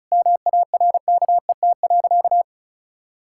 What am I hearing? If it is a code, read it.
Morse code, heard as a keyed tone, about 36 wpm, 700 Hz. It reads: MARKET.
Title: Sentences from Top 1000 Words - Review Part 6 - 35wpm